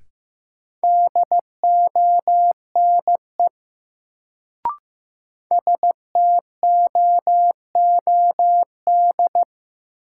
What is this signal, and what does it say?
Morse code, heard as a keyed tone, about 15 wpm, 700 Hz. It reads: DONE STOOD
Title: Sentences from Top 500 Words - New Word Drill - 15wpm